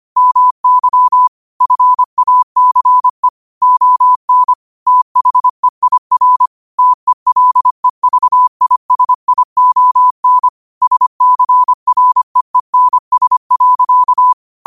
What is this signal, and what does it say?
Morse code, heard as a keyed tone, about 25 wpm, 1000 Hz. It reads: MY FACE ON THEIR TELEVISION SCREENS.